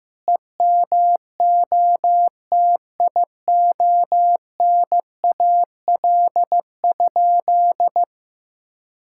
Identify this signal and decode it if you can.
Morse code, heard as a keyed tone, about 15 wpm, 700 Hz. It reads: EMOTIONAL?
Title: Sentences from Top 1000 Words - Review Part 5 - 15wpm